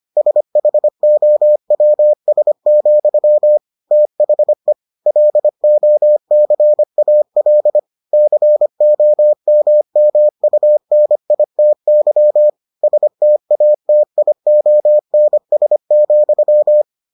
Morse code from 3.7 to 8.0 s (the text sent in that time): THE LOCAL